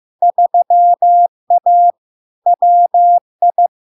Morse code, heard as a keyed tone, about 15 wpm, 700 Hz. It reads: 3A WI
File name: LARK-FD-2022-Log-in-CW-15wpm-700-Hz